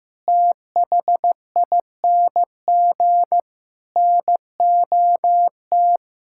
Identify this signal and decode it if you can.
Morse code, heard as a keyed tone, about 16 wpm, 700 Hz. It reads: THING NOT
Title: Sentences from Top 1000 Words - Review Part 6 - 15wpm